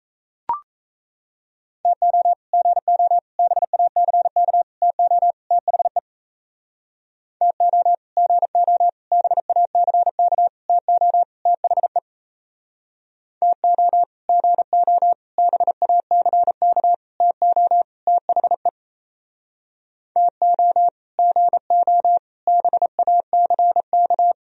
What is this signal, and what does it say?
Morse code, heard as a keyed tone, about 37 wpm, 700 Hz. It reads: TO GO BACK TO THE TO GO BACK TO THE TO GO BACK TO THE TO GO BACK